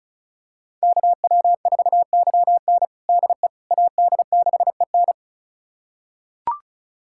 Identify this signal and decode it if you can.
Morse code, heard as a keyed tone, about 36 wpm, 700 Hz. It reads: KW4YN DE AD6EN
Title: USA Callsign DE Callsign - 5 Characters - 35wpm